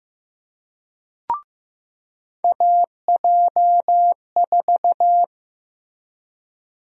Morse code, heard as a keyed tone, about 15 wpm, 700 Hz. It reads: AJ4
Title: USA Call Sign Prefixes - Repeated 2x - 15wpm